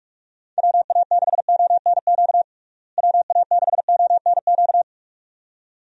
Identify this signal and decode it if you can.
Morse code, heard as a keyed tone, about 42 wpm, 700 Hz. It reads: WA6ONQ WA6ONQ
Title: USA Call Signs - 6 Characters - Repeated 2x - 45wpm